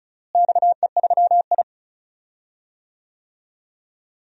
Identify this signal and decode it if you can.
Morse code, heard as a keyed tone, about 36 wpm, 700 Hz. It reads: XE3I
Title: DXCC-100 Call Signs - 4 Characters - Repeated 3x - 35wpm